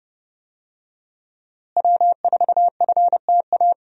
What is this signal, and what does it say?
Morse code, heard as a keyed tone, about 30 wpm, 700 Hz. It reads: W4FTA
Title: USA Call Signs - 5 Characters - Repeated 2x - 30wpm